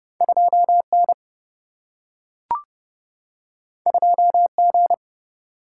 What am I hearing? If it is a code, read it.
Morse code, heard as a keyed tone, about 30 wpm, 700 Hz. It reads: 2 N 2 G
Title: Single-Letter Single-Number 30wpm